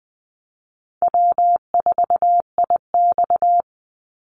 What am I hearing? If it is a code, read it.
Morse code, heard as a keyed tone, about 20 wpm, 700 Hz. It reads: W4IX
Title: Easy Call Signs Repeated 3x - 20wpm